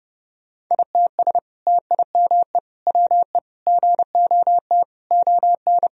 Morse code, heard as a keyed tone, about 32 wpm, 700 Hz. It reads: ITS TIME WE GOT ON
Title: Sentences from Top 1000 Words - Review Part 4 - 30wpm